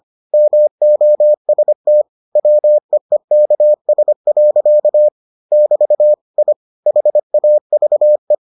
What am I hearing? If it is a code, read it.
Morse code, heard as a keyed tone, about 25 wpm, 600 Hz. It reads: MOST WEEKS. <BT> I HAVE